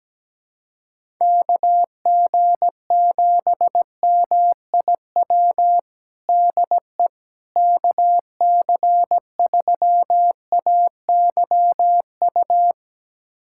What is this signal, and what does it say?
Morse code, heard as a keyed tone, about 17 wpm, 700 Hz. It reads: KG7MIW DE KC3AYU
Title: USA Callsign DE Callsign 17wpm